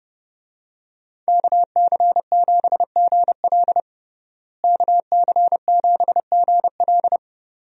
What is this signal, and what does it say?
Morse code, heard as a keyed tone, about 30 wpm, 700 Hz. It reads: KC7GL KC7GL
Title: USA Call Signs - 5 Characters - Repeated 2x - 30wpm